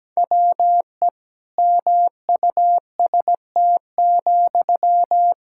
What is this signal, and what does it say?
Morse code, heard as a keyed tone, about 17 wpm, 700 Hz. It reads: WE MUST,
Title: Sentences from Top 1600 Words 17wpm